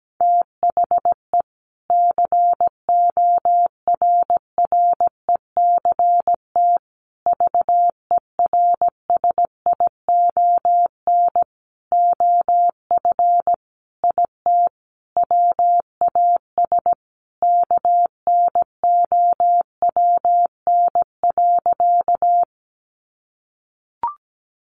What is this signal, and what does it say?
Morse code, heard as a keyed tone, about 17 wpm, 700 Hz. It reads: THE CORRECT VERSION OF IT WAS KNOWN.
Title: Sentences from Top 1400 Words 17wpm